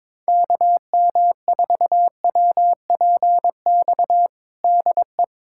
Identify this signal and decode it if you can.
Morse code, heard as a keyed tone, about 21 wpm, 700 Hz. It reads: KM4WPX DE
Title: USA Callsign DE Callsign - 6 Characters - 22wpm